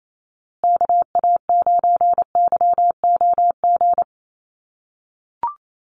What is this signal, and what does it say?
Morse code, heard as a keyed tone, about 28 wpm, 700 Hz. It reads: KA9YOG
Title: USA Call Signs Repeated 2x 28wpm